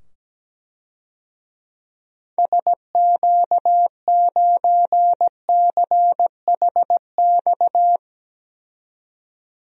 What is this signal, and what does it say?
Morse code, heard as a keyed tone, about 17 wpm, 700 Hz. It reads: SQ9CHX
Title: DXCC-100 Call Signs - 6 Characters - 17wpm